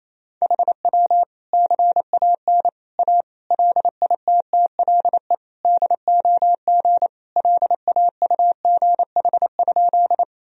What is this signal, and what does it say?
Morse code, heard as a keyed tone, about 28 wpm, 700 Hz. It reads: HW CAN A LITTLE DOG LAUGH?
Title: Sentences from Top 500 Words and Abbreviations 28wpm